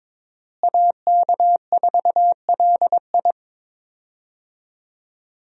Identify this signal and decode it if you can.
Morse code, heard as a keyed tone, about 23 wpm, 700 Hz. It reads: AK4LI
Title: USA Call Signs - 5 Characters - Repeated 3x - 22wpm